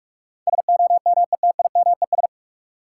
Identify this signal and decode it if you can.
Morse code, heard as a keyed tone, about 46 wpm, 700 Hz. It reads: SOMETIMES
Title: Sentences from Top 600 Words - New Word Drill - 45wpm